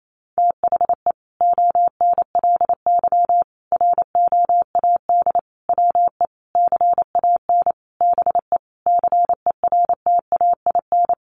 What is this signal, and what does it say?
Morse code, heard as a keyed tone, about 28 wpm, 700 Hz. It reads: THE ONLY ROAD WE CAN BE CERTAIN
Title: Sentences from Top 1100 Words 28wpm